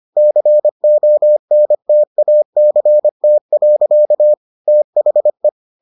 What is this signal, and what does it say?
Morse code, heard as a keyed tone, about 25 wpm, 600 Hz. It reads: CONTACT. THE